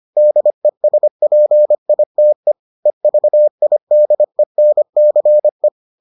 Morse code, heard as a keyed tone, about 25 wpm, 600 Hz. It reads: DESPITE EVIDENCE